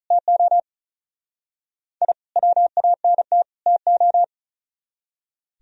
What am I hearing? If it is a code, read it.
Morse code, heard as a keyed tone, about 45 wpm, 700 Hz. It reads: TO I WANT TO